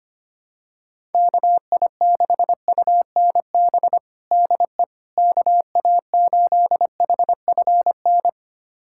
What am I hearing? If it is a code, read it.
Morse code, heard as a keyed tone, about 25 wpm, 700 Hz. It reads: KI6UNB DE KA8HFN
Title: USA Callsign DE Callsign - 6 Characters - 25wpm